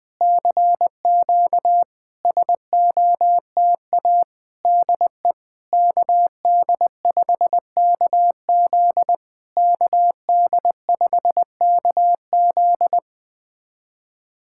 Morse code, heard as a keyed tone, about 20 wpm, 700 Hz. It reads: CQ SOTA DE KD5KZ KD5KZ
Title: SOTA QSO Practice 20wpm